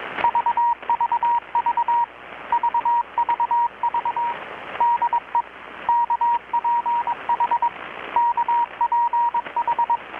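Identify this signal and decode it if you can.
Morse code, heard as a keyed tone, about 22 wpm, 950 Hz, with radio noise. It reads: VVV VVV DE KPH KPH
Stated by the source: Morse from KPH's Night of Nights XX maritime radiotelegraphy event, received in Chiapas, Mexico about 3,800 km away on a Kenwood R-600 with a grounded 10 m long-wire antenna.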